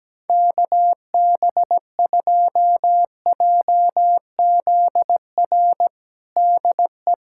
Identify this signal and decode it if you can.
Morse code, heard as a keyed tone, about 17 wpm, 700 Hz. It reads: KB2JZR DE
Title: USA Callsign DE Callsign - 6 Characters - 17wpm